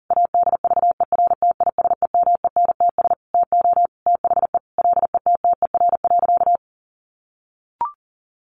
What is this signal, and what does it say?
Morse code, heard as a keyed tone, about 40 wpm, 700 Hz. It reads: ADVERTISEMENTS TO THE LETTER.
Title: A Sound Of Thunder 40wpm